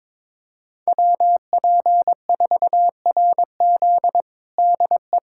Morse code, heard as a keyed tone, about 22 wpm, 700 Hz. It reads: WP4RZ DE
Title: USA Callsign DE Callsign - 5 Characters - 22wpm